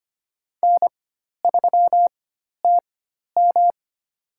Morse code, heard as a keyed tone, about 25 wpm, spaced as for 12 wpm, 700 Hz. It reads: N3TM
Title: USA Call Signs - 4x Character Spacing - 25wpm